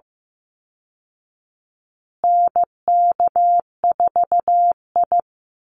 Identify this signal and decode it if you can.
Morse code, heard as a keyed tone, about 15 wpm, 700 Hz. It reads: NK4I
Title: Call Signs Repeated 2x - 15wpm